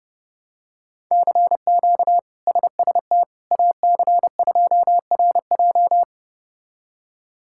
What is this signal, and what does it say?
Morse code, heard as a keyed tone, about 30 wpm, 700 Hz. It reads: CQ SST AC2RJ
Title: K1USN SST QSO Practice - 30wpm